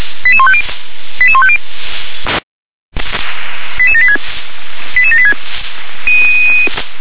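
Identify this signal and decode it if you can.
Five-tone selective call (ZVEI), heard as quick runs of five tones, tones of about 80 ms. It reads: 88150 88150 88786 88786
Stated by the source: ZVEI five-tone selective call 88786 received by SDR.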